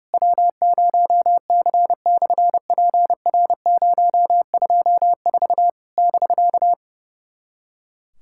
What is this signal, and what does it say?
Morse code, heard as a keyed tone, about 30 wpm, 690 Hz. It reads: W0C/PR024 <BK>
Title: SOTA QSO Practice 30wpm